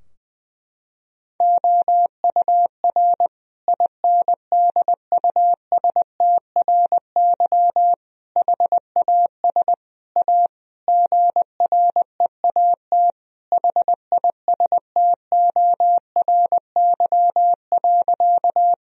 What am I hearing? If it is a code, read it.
Morse code, heard as a keyed tone, about 21 wpm, 700 Hz. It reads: OUR INDUSTRY HAS A GREAT HISTORY.
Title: Sentences from Top 1000 Words - Review Part 4 - 20wpm